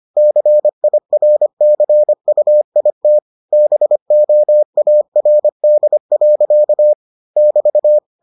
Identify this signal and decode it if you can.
Morse code, heard as a keyed tone, about 25 wpm, 600 Hz. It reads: CIRCUIT BOARD. <BT>